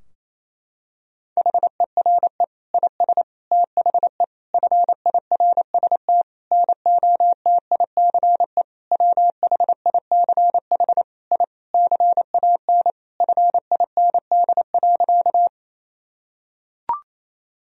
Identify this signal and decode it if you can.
Morse code, heard as a keyed tone, about 28 wpm, 700 Hz. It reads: HERE IS THE FIRST NOTICE WHICH I CAN FIND.
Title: Sentences from Top 1000 Words - Review Part 2 - 28wpm